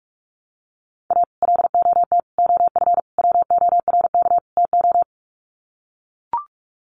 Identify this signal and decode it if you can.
Morse code, heard as a keyed tone, about 44 wpm, 700 Hz. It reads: A LOT OF WORK TO
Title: Sets of 5 Words Encore2 45wpm